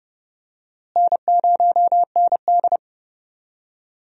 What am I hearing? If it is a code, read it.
Morse code, heard as a keyed tone, about 31 wpm, 700 Hz. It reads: N0ND
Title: USA Call Signs - 4 Characters - Repeated 3x - 30wpm